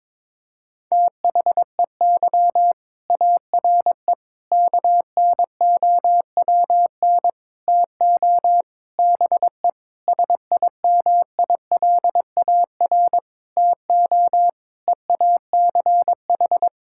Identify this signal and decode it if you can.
Morse code, heard as a keyed tone, about 22 wpm, 700 Hz. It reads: THEY ARE KNOWN TO BE SIMILAR TO EACH
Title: Sentences from Top 1000 Words - Review Part 5 - 22wpm